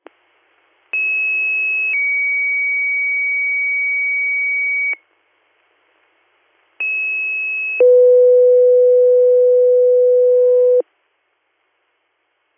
Two-tone page, heard as a one-second tone, then a three-second tone, about 2600 then 2250 Hz, about 2600 then 500 Hz. It is heard from 0.9 to 10.8 s.